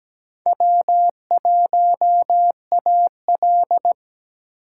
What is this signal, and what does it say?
Morse code, heard as a keyed tone, about 20 wpm, 700 Hz. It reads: W1AL
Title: Speed Racing - USA Call Signs 17wpm